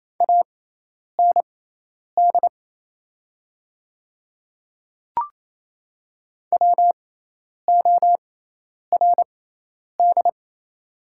Morse code, heard as a keyed tone, about 27 wpm, spaced as for 10 wpm, 690 Hz. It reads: AND WORD
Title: 6x Character Spacing - Top 100 Words - 28wpm